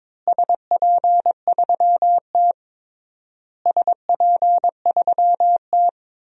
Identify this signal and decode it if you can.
Morse code, heard as a keyed tone, about 22 wpm, 700 Hz. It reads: SP3T SP3T
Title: DXCC-100 Call Signs - 4 Characters - Repeated 3x - 22wpm